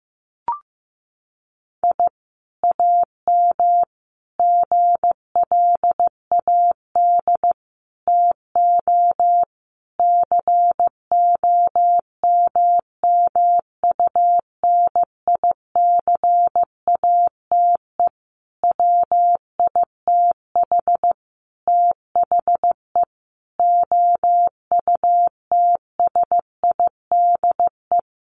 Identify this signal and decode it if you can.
Morse code, heard as a keyed tone, about 15 wpm, 700 Hz. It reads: I AM GLAD TO COMMUNICATE WITH THE OUTSIDE
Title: Sentences from Top 1300 Words 15wpm